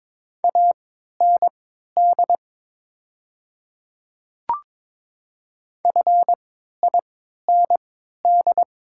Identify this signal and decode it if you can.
Morse code, heard as a keyed tone, about 22 wpm, spaced as for 12 wpm, 690 Hz. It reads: AND FIND
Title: 3x Character Spacing - Top 200 Words - 22wpm